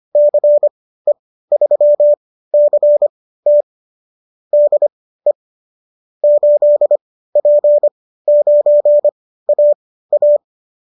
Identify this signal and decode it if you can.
Morse code, heard as a keyed tone, about 25 wpm, spaced as for 15 wpm, 590 Hz. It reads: CE3CT DE 8P9AA